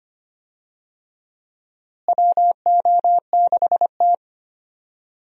Morse code, heard as a keyed tone, about 25 wpm, 700 Hz. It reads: WO6T
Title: USA Call Signs - 4 Characters - Repeated 3x - 25wpm